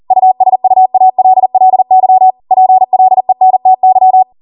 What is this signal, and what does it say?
Morse code, heard as a keyed tone, about 40 wpm, 750 Hz. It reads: USUALLY PLENTY